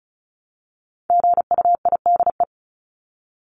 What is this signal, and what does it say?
Morse code, heard as a keyed tone, about 34 wpm, 700 Hz. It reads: GUIDE
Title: Top 501-750 Words 35wpm